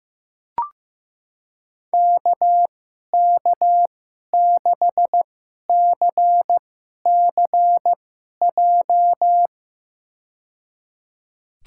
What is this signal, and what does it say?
Morse code, heard as a keyed tone, about 15 wpm, spaced as for 11 wpm, 700 Hz. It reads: KK6CCJ
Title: USA Call Signs - 6 Characters - 2x Character Spacing - 15wpm